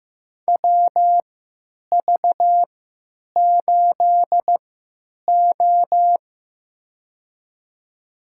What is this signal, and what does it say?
Morse code, heard as a keyed tone, about 15 wpm, spaced as for 9 wpm, 700 Hz. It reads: WV8O
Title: USA Call Signs - 3x Character Spacing - 15wpm